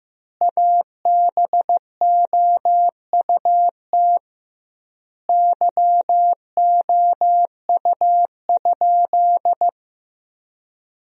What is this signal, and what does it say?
Morse code, heard as a keyed tone, about 15 wpm, 700 Hz. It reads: ABOUT YOU?
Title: 2x Word-Spacing - Sentences from Top 100 Words - 15wpm